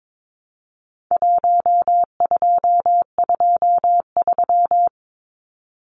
Morse code, heard as a keyed tone, about 22 wpm, 700 Hz. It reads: 1223
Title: Numbers - Contest Style - 4-Digit 22wpm